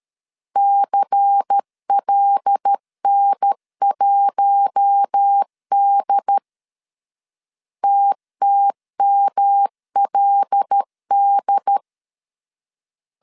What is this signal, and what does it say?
Morse code, heard as a keyed tone, about 13 wpm, 800 Hz. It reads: CLN1D TTMLD
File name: CW / DET / 1 6 1500C S A 12wpm 3440